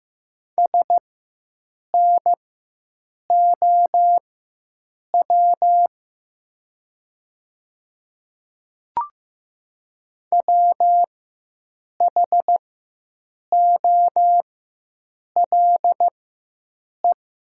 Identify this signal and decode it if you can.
Morse code, heard as a keyed tone, about 15 wpm, spaced as for 7 wpm, 700 Hz. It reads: SNOW WHOLE
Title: 4x Character Spacing - Top 500 Words - 15wpm